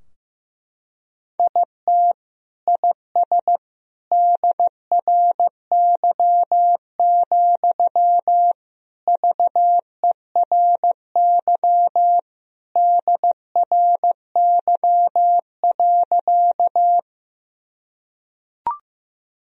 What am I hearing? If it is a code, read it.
Morse code, heard as a keyed tone, about 15 wpm, 700 Hz. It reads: IT IS DRY, VERY DRY.
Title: Sentences from Top 1000 Words - Review Part 3 - 15wpm